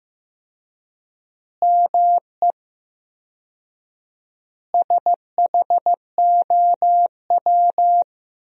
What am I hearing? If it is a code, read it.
Morse code, heard as a keyed tone, about 15 wpm, 700 Hz. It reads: ME SHOW
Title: 4x Word-Spacing - Sentences from Top 500 Words - 15wpm